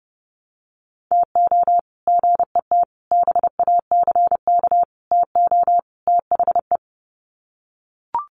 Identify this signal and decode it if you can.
Morse code, heard as a keyed tone, about 30 wpm, 700 Hz. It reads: TO GET BACK TO THE
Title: Sets of 5 Words Encore 30wpm